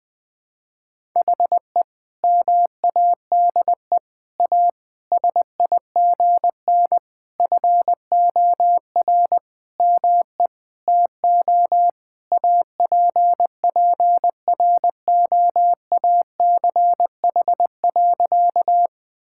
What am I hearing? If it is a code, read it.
Morse code, heard as a keyed tone, about 21 wpm, 700 Hz. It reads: HE MADE A SIGN FOR ME TO APPROACH.
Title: Sentences from Top 1000 Words - Review Part 6 - 20wpm